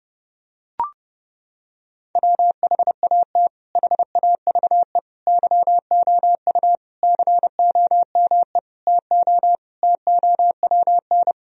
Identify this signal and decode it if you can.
Morse code, heard as a keyed tone, about 30 wpm, 700 Hz. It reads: WHAT HAVE YOU COME TO TOWN
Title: Sentences from Top 1000 Words - Review Part 2 - 30wpm